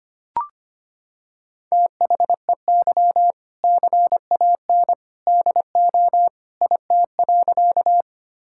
Morse code, heard as a keyed tone, about 25 wpm, 700 Hz. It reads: THEY CAN DO IT.
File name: practice / 25wpm / Sentences with 2-Word Phrases - 25wpm